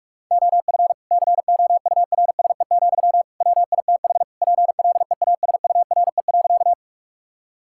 Morse code, heard as a keyed tone, about 46 wpm, 700 Hz. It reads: OF COURSE, WITH PLEASURE.
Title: Sentences from Top 1600 Words 45wpm